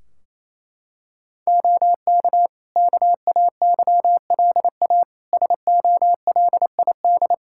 Morse code, heard as a keyed tone, about 28 wpm, 700 Hz. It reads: OK KAYLA SOLID
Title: K1USN SST QSO Practice 28wpm